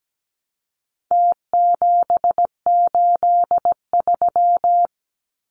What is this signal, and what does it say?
Morse code, heard as a keyed tone, about 17 wpm, 700 Hz. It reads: T783
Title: Numbers - Contest Style - 4-Digit 17wpm